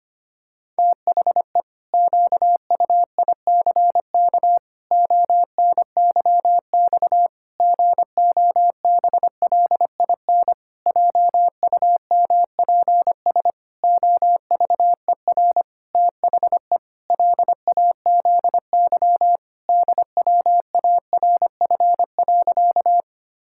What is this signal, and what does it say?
Morse code, heard as a keyed tone, about 25 wpm, 700 Hz. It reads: THE QUICK ONYX GOBLIN JUMPS OVER THE LAZY DWARF.
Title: Pangram Challenge v2 - 25wpm